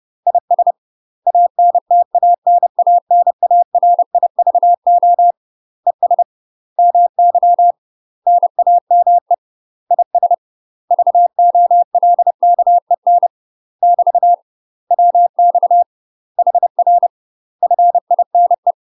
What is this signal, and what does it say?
Morse code, heard as a keyed tone, about 30 wpm, 700 Hz. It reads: IS ANTANANARIVO ES MY NAME IS VOLKEN <BT> WX HR FINE
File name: UK MASTER QSOs TRAINING/30WPM